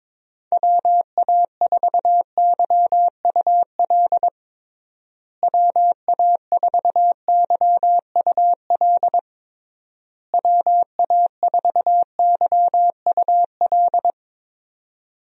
Morse code, heard as a keyed tone, about 22 wpm, 700 Hz. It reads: WA4YUL WA4YUL WA4YUL
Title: USA Call Signs - 6 Characters - Repeated 3x - 22wpm